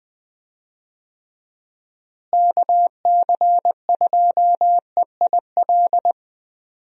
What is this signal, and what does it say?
Morse code, heard as a keyed tone, about 20 wpm, 700 Hz. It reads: KC2EIL
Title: USA Call Signs - 6 Characters - Repeated 2x - 20wpm